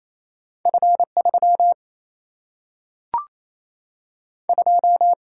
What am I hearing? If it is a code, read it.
Morse code, heard as a keyed tone, about 28 wpm, 700 Hz. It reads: F 3 2
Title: Single-Letter Single-Number 28wpm